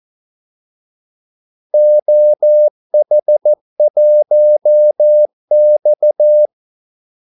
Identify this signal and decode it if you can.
Morse code, heard as a keyed tone, about 14 wpm, 600 Hz. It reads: OH1X
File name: Callsigns-14wpm